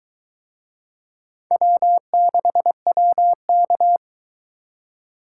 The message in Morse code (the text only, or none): W6WK